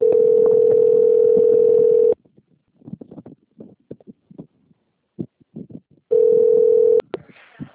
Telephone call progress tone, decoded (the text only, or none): ringback tone